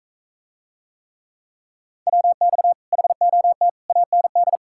Morse code, heard as a keyed tone, about 40 wpm, 700 Hz. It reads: WX HOT AND